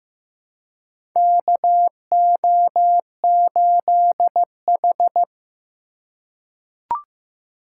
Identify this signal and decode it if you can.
Morse code, heard as a keyed tone, about 15 wpm, 700 Hz. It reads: KO8H
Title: USA Call Signs - 4 Characters - 3x Character Spacing - 15wpm